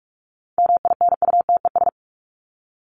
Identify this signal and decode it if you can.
Morse code, heard as a keyed tone, about 44 wpm, 700 Hz. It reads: MINUTES
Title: ES:S Suffix - Top 100 Words with Base Words 45wpm